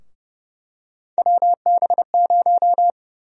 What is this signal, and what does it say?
Morse code, heard as a keyed tone, about 32 wpm, 700 Hz. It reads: WB0
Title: USA Call Sign Prefixes - 30wpm